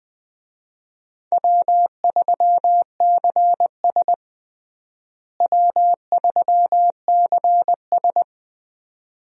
Morse code, heard as a keyed tone, about 20 wpm, 700 Hz. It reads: W3CS W3CS
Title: USA Call Signs - 4 Characters - Repeated 2x - 20wpm